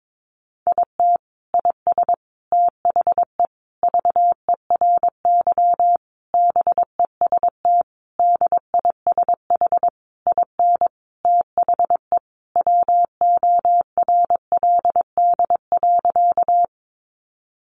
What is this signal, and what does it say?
Morse code, heard as a keyed tone, about 22 wpm, 700 Hz. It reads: IT IS THE VERY BEST DISH IN THE WORLD.
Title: Sentences from Top 1300 Words 22wpm